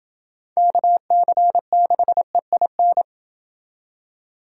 Morse code, heard as a keyed tone, about 28 wpm, 700 Hz. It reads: KC6EIN